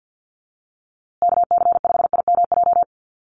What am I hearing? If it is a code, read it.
Morse code, heard as a keyed tone, about 48 wpm, 700 Hz. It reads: KC5IMJ
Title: USA Call Signs Repeated 3x 50wpm